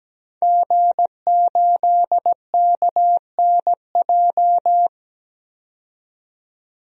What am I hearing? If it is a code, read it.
Morse code, heard as a keyed tone, about 17 wpm, 700 Hz. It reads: G8KNJ
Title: DXCC-100 Call Signs - 5 Characters - 17wpm